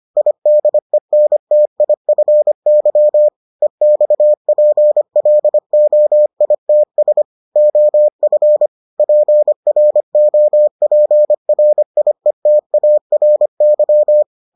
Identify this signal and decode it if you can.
Morse code, heard as a keyed tone, about 25 wpm, 600 Hz. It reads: IDENTIFY EXPLOITS OF PROPRIETARY